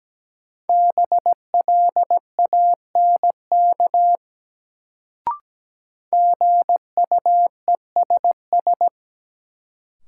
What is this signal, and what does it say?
Morse code, heard as a keyed tone, about 17 wpm, 700 Hz. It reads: BLANK GUESS
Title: Sentences from Top 1500 Words - New Word Drill Alt - 17wpm